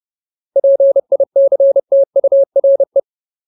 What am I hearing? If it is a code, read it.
Morse code, heard as a keyed tone, about 30 wpm, 550 Hz. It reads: PICTURE
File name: Ultimate Word List / 30WPM Ultimate